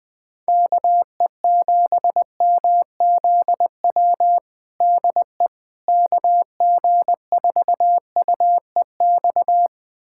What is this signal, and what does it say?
Morse code, heard as a keyed tone, about 20 wpm, 700 Hz. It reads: KE7MZW DE KG4UEX
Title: USA Callsign DE Callsign - 6 Characters - 20wpm